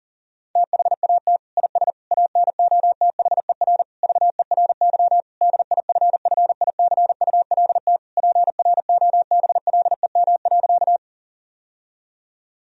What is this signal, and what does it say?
Morse code, heard as a keyed tone, about 40 wpm, 700 Hz. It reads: THAT IS ANOTHER VERY DIFFICULT PROBLEM.
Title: Sentences from Top 1000 Words - Review Part 2 - 40wpm